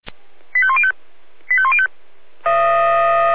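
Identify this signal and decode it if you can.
Five-tone selective call (ZVEI), heard as quick runs of five tones, tones of about 70 ms. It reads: 85105 85105
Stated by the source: ZVEI five-tone call 85300 received on an SDR.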